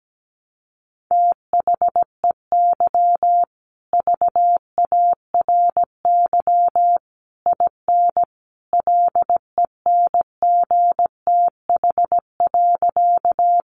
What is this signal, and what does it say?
Morse code, heard as a keyed tone, about 17 wpm, 700 Hz. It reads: THEY VARY IN LENGTH.
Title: Sentences from Top 1200 Words 17wpm